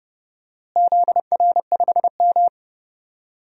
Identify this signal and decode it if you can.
Morse code, heard as a keyed tone, about 30 wpm, 700 Hz. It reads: ZR5M
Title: DXCC-100 Call Signs - 4 Characters - Repeated 3x - 30wpm